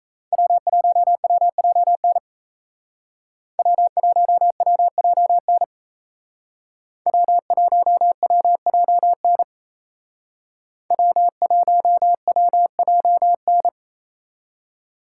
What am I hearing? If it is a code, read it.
Morse code, heard as a keyed tone, about 33 wpm, 700 Hz. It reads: W1WJN W1WJN W1WJN W1WJN
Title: Speed Racing - USA 1x3 Call Signs 28wpm